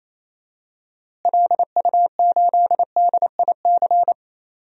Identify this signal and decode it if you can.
Morse code, heard as a keyed tone, about 27 wpm, 700 Hz. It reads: LU8DIC
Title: DXCC-100 Call Signs - 6 Characters - Repeated 2x - 28wpm